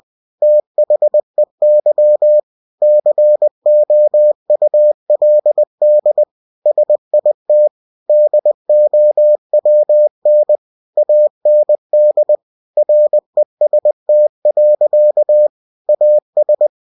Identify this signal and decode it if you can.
Morse code, heard as a keyed tone, about 20 wpm, 600 Hz. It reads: THEY COULD SIT DOWN AND REST. AS
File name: cw-books / winnie-the-pooh / winnie-20wpm-page-0083